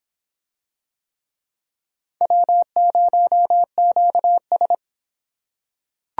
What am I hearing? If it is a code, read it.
Morse code, heard as a keyed tone, about 24 wpm, 700 Hz. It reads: W0QS